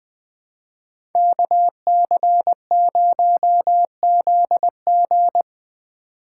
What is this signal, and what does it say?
Morse code, heard as a keyed tone, about 22 wpm, 700 Hz. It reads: KC0ZG
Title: Speed Racing - USA 2x2 Call Signs 20wpm